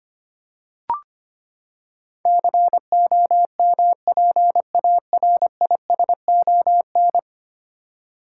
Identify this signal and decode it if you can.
Morse code, heard as a keyed tone, about 25 wpm, 700 Hz. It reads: COMPARISON
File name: Ten Letter Words - 25wpm